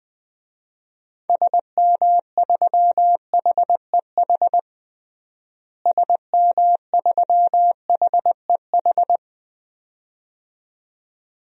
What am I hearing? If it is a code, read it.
Morse code, heard as a keyed tone, about 21 wpm, 700 Hz. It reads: SM3HEH SM3HEH
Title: DXCC-100 Call Signs - 6 Characters - Repeated 2x - 20wpm